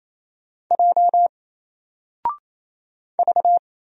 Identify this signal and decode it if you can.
Morse code, heard as a keyed tone, about 27 wpm, 700 Hz. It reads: J V
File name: Single-Letter-Number-28wpm